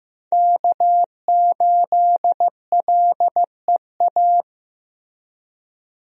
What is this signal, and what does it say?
Morse code, heard as a keyed tone, about 15 wpm, 700 Hz. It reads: K8LEA
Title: USA Call Signs - 5 Characters - Repeated 3x - 15wpm